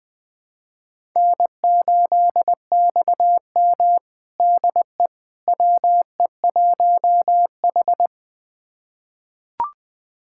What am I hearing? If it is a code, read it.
Morse code, heard as a keyed tone, about 21 wpm, 700 Hz. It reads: N8XM DE WE1H
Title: USA Callsign DE Callsign - 4 Characters - 20wpm